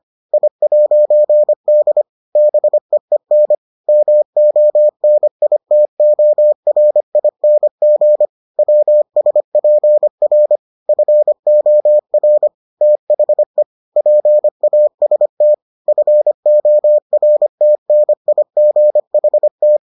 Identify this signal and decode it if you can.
Morse code, heard as a keyed tone, about 25 wpm, 600 Hz. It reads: I'D BEEN MONITORING WSPR FOR THE PAST FORTNIGHT